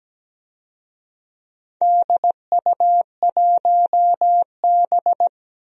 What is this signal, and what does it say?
Morse code, heard as a keyed tone, about 17 wpm, 700 Hz. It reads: DU1B
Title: DXCC-100 Call Signs - 4 Characters - Repeated 3x - 17wpm